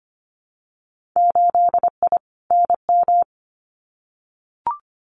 Morse code, heard as a keyed tone, about 25 wpm, 700 Hz. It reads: 8I NM
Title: Winter Field Day Exchanges Encore 25wpm